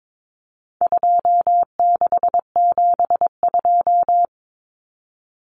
Morse code, heard as a keyed tone, about 22 wpm, 700 Hz. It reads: 2672
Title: Numbers - 4-Digit 22wpm